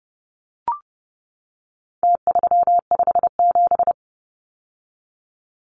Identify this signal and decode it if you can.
Morse code, heard as a keyed tone, about 30 wpm, 700 Hz. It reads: T357
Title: Numbers - Contest Style - 4-Digit 30wpm